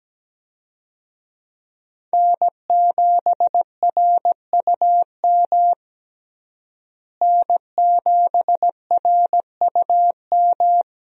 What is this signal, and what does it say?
Morse code, heard as a keyed tone, about 17 wpm, 700 Hz. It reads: N7RUM N7RUM
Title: USA Call Signs - 5 Characters - Repeated 2x - 17wpm